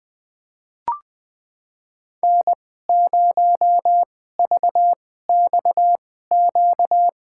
Morse code, heard as a keyed tone, about 20 wpm, spaced as for 15 wpm, 700 Hz. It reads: N0VXQ
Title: USA Call Signs - 5 Characters - 2x Character Spacing - 20wpm